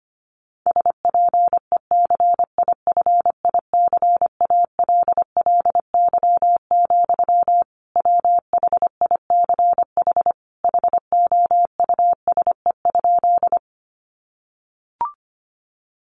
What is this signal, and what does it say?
Morse code, heard as a keyed tone, about 25 wpm, 700 Hz. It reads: SPECIFICALLY, WHICH HOUSE?
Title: Sentences from Top 1100 Words 25wpm